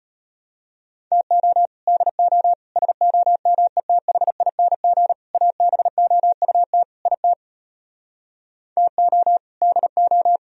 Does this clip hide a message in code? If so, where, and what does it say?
Yes: Morse code from 1.1 to 10.4 s, heard as a keyed tone, about 33 wpm, 700 Hz. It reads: TO DO SOMETHING ABOUT IT TO DO